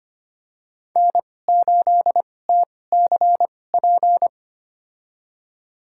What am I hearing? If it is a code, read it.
Morse code, heard as a keyed tone, about 25 wpm, spaced as for 18 wpm, 700 Hz. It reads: N8TCP
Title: USA Call Signs - 5 Characters - 2x Character Spacing - 25wpm